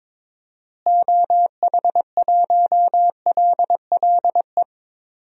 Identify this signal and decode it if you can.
Morse code, heard as a keyed tone, about 22 wpm, 700 Hz. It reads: OH1LLE
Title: DXCC-100 Call Signs - 6 Characters - Repeated 2x - 22wpm